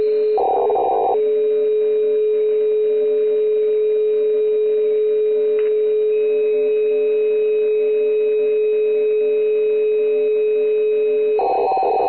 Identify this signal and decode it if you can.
Dial tone (Europe, 425 Hz), with noise.